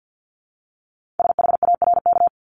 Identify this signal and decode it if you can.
Morse code, heard as a keyed tone, about 50 wpm, 700 Hz. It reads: SHARK
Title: Multi-letter Phonemes 50wpm